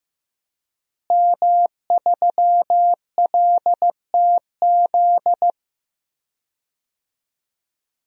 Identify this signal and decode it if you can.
Morse code, heard as a keyed tone, about 15 wpm, 700 Hz. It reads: M3LTZ
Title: DXCC-100 Call Signs - 5 Characters - Repeated 3x - 15wpm